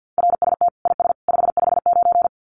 Morse code, heard as a keyed tone, about 50 wpm, 700 Hz. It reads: RST IS 559
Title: Common QSO Phrases 50wpm